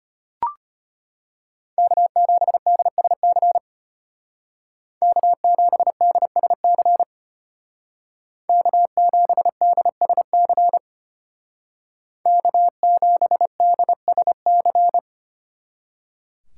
Morse code, heard as a keyed tone, about 31 wpm, 700 Hz. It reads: K7DSC K7DSC K7DSC K7DSC
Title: Speed Racing - USA Call Signs 25wpm